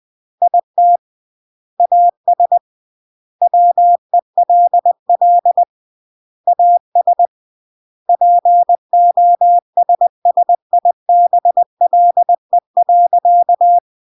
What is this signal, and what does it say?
Morse code, heard as a keyed tone, about 20 wpm, 700 Hz. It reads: IT AS WELL AS POSSIBLE.
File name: CW / jodorowsky / 82 commandments 20-20